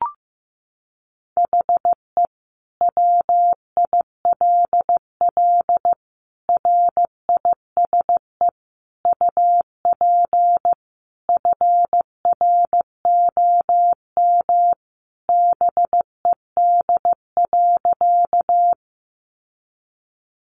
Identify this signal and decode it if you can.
Morse code, heard as a keyed tone, about 15 wpm, 700 Hz. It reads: HE WILL RISE UP FROM BED.
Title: Sentences from Top 900 Words 15wpm